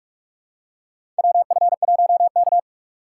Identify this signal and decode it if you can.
Morse code, heard as a keyed tone, about 45 wpm, 700 Hz. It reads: WF1K
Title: USA Call Signs - 4 Characters - 3x Character Spacing - 45wpm